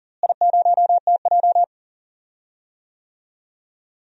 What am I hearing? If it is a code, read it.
Morse code, heard as a keyed tone, about 40 wpm, 700 Hz. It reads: I0TJ
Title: DXCC-100 Call Signs - 4 Characters - Repeated 3x - 40wpm